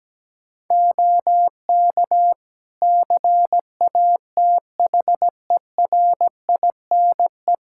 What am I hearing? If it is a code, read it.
Morse code, heard as a keyed tone, about 17 wpm, 700 Hz. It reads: OK CATHERINE
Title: Practice QSO - Exchange 2 - Info - 17wpm